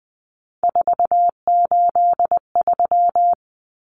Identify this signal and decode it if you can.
Morse code, heard as a keyed tone, about 20 wpm, 700 Hz. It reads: 483
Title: Numbers - 3-Digit 20wpm